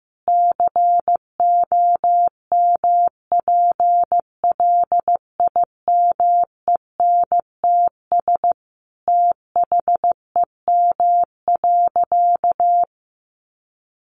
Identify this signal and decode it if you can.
Morse code, heard as a keyed tone, about 15 wpm, 700 Hz. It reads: COMPLIMENTS THEM.
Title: Easy Phrases 15wpm